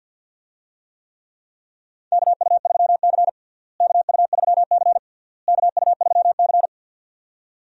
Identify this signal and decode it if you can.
Morse code, heard as a keyed tone, about 50 wpm, 700 Hz. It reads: KU3C KU3C KU3C
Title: USA Call Signs - 4 Characters - Repeated 3x - 50wpm